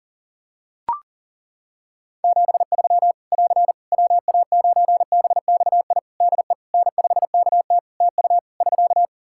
Morse code, heard as a keyed tone, about 40 wpm, 700 Hz. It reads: 73 <AR> WA9BXI DE N5KT TU <SK>
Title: Practice QSO - Exchange 4 - Ending - 40wpm